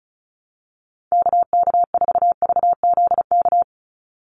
Morse code, heard as a keyed tone, about 34 wpm, 700 Hz. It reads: KK4VZK
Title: USA Call Signs 35wpm